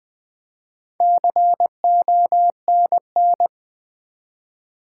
Morse code, heard as a keyed tone, about 20 wpm, 700 Hz. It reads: CONN
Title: Build A Word - 4 Letters At A Time - 10-Letter Words - 20wpm